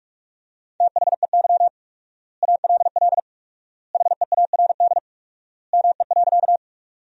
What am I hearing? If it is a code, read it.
Morse code, heard as a keyed tone, about 46 wpm, 700 Hz. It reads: THEY ALL HEARD ME.
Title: 4x Word-Spacing - Sentences from Top 400 Words - 45wpm